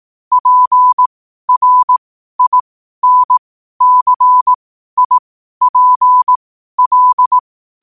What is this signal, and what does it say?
Morse code, heard as a keyed tone, about 18 wpm, spaced as for 12 wpm, 1000 Hz. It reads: PRINCIPL
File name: qotdfiles / 12wpm